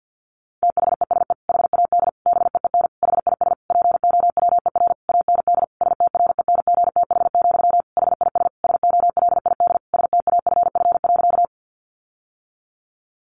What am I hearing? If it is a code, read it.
Morse code, heard as a keyed tone, about 50 wpm, 700 Hz. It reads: THESE HAD BEEN HIS POWER AND STRENGTH, HIS SOLID STAFF.
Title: Sentences from Top 1500 Words 50wpm